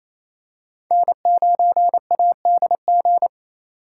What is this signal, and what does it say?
Morse code, heard as a keyed tone, about 27 wpm, 700 Hz. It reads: N9ADG
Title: K1USN SST QSO Practice - 28wpm